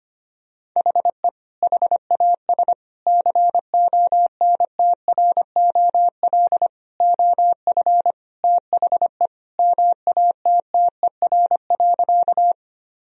Morse code, heard as a keyed tone, about 25 wpm, 700 Hz. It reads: HE HAS CONTROL OF THE MATTER.
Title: Sentences from Top 1000 Words - Review Part 3 - 25wpm